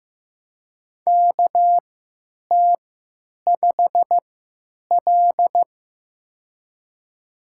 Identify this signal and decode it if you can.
Morse code, heard as a keyed tone, about 15 wpm, spaced as for 8 wpm, 700 Hz. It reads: KT5L
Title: USA Call Signs - 3x Character Spacing - 15wpm